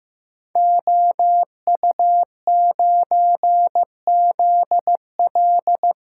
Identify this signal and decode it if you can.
Morse code, heard as a keyed tone, about 15 wpm, 700 Hz. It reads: OU9ZL
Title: Code Groups - 5 Letter:Numbers - 15wpm